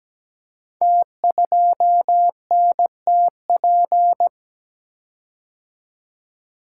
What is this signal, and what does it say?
Morse code, heard as a keyed tone, about 17 wpm, 700 Hz. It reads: T2NTP
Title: Code Groups - 5 Letter:Numbers - 17wpm